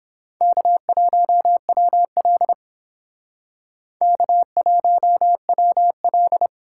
Morse code, heard as a keyed tone, about 30 wpm, 700 Hz. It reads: K1WL K1WL